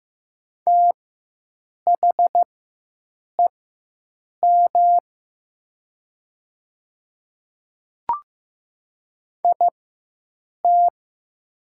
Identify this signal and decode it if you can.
Morse code, heard as a keyed tone, about 15 wpm, spaced as for 7 wpm, 700 Hz. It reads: THEM IT
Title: 4x Character Spacing - Top 100 Words - 15wpm